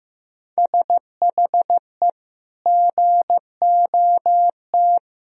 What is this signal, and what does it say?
Morse code, heard as a keyed tone, about 15 wpm, 700 Hz. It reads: SHE GOT
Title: Sentences from Top 500 Words and Abbreviations 15wpm